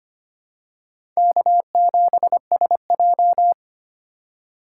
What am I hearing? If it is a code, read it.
Morse code, heard as a keyed tone, about 25 wpm, 700 Hz. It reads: K7SJ
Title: USA Call Signs - 4 Characters - 3x Character Spacing - 25wpm